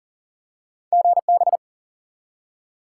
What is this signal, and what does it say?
Morse code, heard as a keyed tone, about 40 wpm, 700 Hz. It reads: GB
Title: Top 90 QSO Elements 40wpm